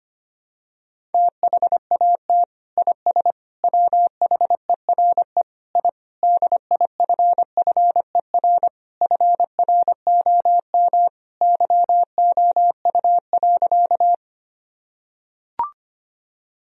Morse code, heard as a keyed tone, about 25 wpm, 700 Hz. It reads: THAT IS WHERE I DIFFER FROM YOU.
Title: Sentences from Top 1000 Words - Review Part 1 - 25wpm